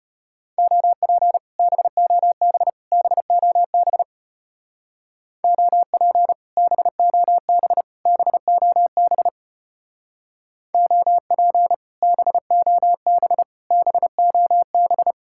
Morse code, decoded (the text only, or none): OP BOB BOB OP BOB BOB OP BOB BOB